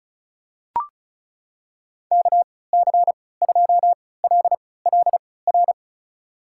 Morse code, heard as a keyed tone, about 36 wpm, spaced as for 20 wpm, 700 Hz. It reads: KC2LLR
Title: USA Call Signs - 6 Characters - 3x Character Spacing - 35wpm